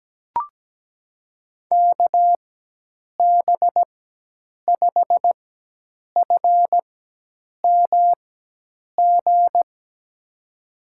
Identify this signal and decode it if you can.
Morse code, heard as a keyed tone, about 17 wpm, spaced as for 8 wpm, 700 Hz. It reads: KB5FMG
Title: USA Call Signs - 6 Characters - 4x Character Spacing - 17wpm